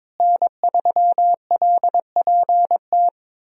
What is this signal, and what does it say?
Morse code, heard as a keyed tone, about 22 wpm, 700 Hz. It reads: N3LPT
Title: USA Call Signs - 5 Characters - Repeated 2x - 22wpm